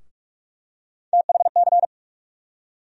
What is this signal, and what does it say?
Morse code, heard as a keyed tone, about 46 wpm, 700 Hz. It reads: THC
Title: USA Call Sign Postfixes - 3 Letters 45wpm